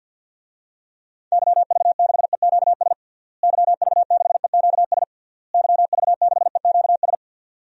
Morse code, heard as a keyed tone, about 52 wpm, 700 Hz. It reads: YV6EQS YV6EQS YV6EQS
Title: DXCC-100 Call Signs - 6 Characters - Repeated 3x - 50wpm